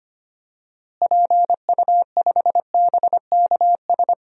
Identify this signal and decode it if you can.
Morse code, heard as a keyed tone, about 25 wpm, 700 Hz. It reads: PU5BKS
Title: DXCC-100 Call Signs - 6 Characters - Repeated 3x - 25wpm